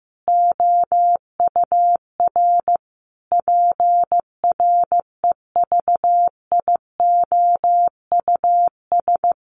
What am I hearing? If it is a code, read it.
Morse code, heard as a keyed tone, about 15 wpm, 700 Hz. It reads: OUR PREVIOUS
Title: Sentences from Top 1500 Words 15wpm